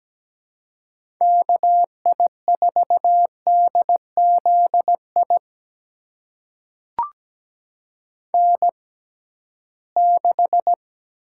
Morse code, heard as a keyed tone, about 17 wpm, spaced as for 13 wpm, 700 Hz. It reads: KI4DZI N6
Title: USA Call Signs - 6x Character Spacing - 17wpm